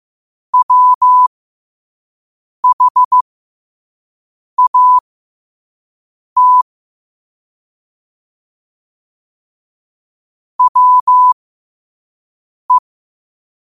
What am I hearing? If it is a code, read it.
Morse code, heard as a keyed tone, about 15 wpm, spaced as for 5 wpm, 1000 Hz. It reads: WHAT WE
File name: qotdfiles / 5wpm